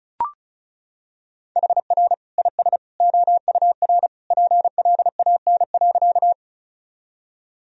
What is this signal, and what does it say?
Morse code, heard as a keyed tone, about 36 wpm, 700 Hz. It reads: HR IS OUR PLAN.
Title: Sentences from Top 500 Words and Abbreviations 35wpm